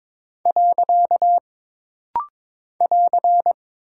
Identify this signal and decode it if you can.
Morse code, heard as a keyed tone, about 22 wpm, 700 Hz. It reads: . <AR>
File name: practice / 22wpm / Single Punctuation-Prosign - Rapid-Fire - 22wpm